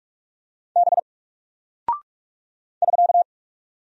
Morse code, heard as a keyed tone, about 44 wpm, 700 Hz. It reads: B <SK>
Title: Single Character (Letter:Number:Punctuation:Prosign) 45wpm